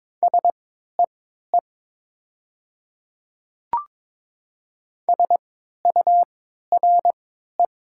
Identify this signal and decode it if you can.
Morse code, heard as a keyed tone, about 22 wpm, spaced as for 12 wpm, 700 Hz. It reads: SEE SURE
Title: 3x Character Spacing - Top 300 Words - 22wpm